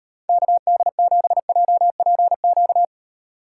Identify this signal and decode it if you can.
Morse code, heard as a keyed tone, about 32 wpm, 700 Hz. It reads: KD7JPQ